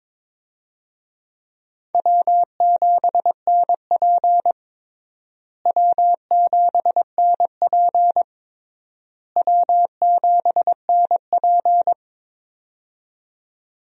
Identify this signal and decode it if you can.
Morse code, heard as a keyed tone, about 22 wpm, 700 Hz. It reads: W7NP W7NP W7NP
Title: USA Call Signs - 4 Characters - Repeated 3x - 22wpm